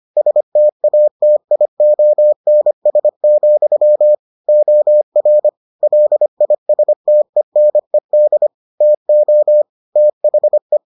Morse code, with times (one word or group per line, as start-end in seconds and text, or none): 0.2-4.2 STATIONS,
4.5-5.5 OR
5.8-8.5 LISTENED
8.8-9.6 TO
10.0-10.8 THE